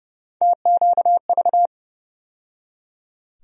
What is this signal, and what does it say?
Morse code, heard as a keyed tone, about 32 wpm, 700 Hz. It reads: TQV
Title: USA Call Sign Postfixes - 3 Letters 30wpm